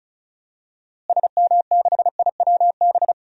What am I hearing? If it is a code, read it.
Morse code, heard as a keyed tone, about 36 wpm, 700 Hz. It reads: SM6IWB
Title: DXCC-100 Call Signs - 6 Characters - Repeated 2x - 35wpm